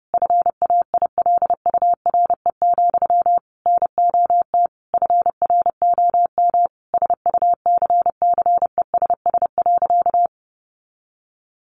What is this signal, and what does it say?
Morse code, heard as a keyed tone, about 30 wpm, 700 Hz. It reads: FAILURE, NOT FROM SUCCESS.
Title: Sentences from Top 900 Words 30wpm